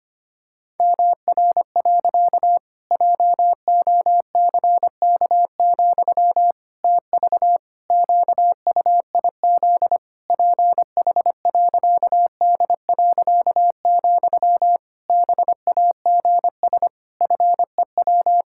Morse code, heard as a keyed tone, about 25 wpm, 700 Hz. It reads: MR. JOCK, TV QUIZ PH.D., BAGS FEW
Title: Pangram Challenge v2 - 25wpm